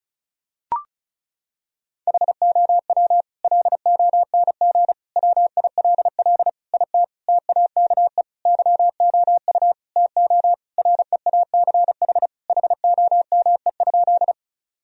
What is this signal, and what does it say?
Morse code, heard as a keyed tone, about 36 wpm, 700 Hz. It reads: HOW LONG WILL IT TAKE YOU TO REACH HOME?
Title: Sentences from Top 1000 Words - Review Part 2 - 35wpm